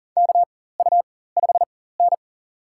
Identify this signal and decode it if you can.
Morse code, heard as a keyed tone, about 40 wpm, spaced as for 19 wpm, 700 Hz. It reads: KU5N
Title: USA Call Signs - 4 Characters - 4x Character Spacing - 40wpm